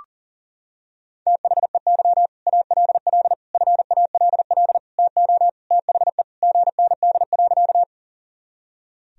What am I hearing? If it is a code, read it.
Morse code, heard as a keyed tone, about 40 wpm, 700 Hz. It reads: THEY ALL FALL TO THE GND.
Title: Sentences from Top 500 Words and Abbreviations 40wpm